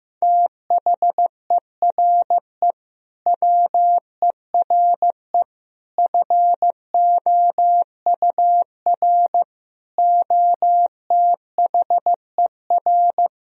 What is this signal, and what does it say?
Morse code, heard as a keyed tone, about 15 wpm, 700 Hz. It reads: THERE WERE FOUR OTHER
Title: Sentences from Top 1000 Words - Review Part 1 - 15wpm